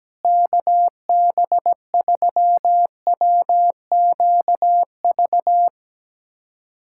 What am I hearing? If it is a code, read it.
Morse code, heard as a keyed tone, about 17 wpm, 700 Hz. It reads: KB3WQV
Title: USA Callsign DE Callsign 17wpm